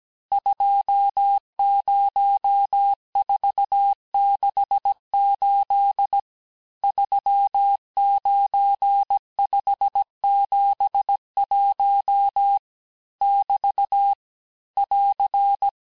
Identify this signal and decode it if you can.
Morse code, heard as a keyed tone, about 17 wpm, 800 Hz. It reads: 20468 39571 <BT> <AR>
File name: cours-du-lundi files / cours-1587-texte-chiffre-vitesse-16MPM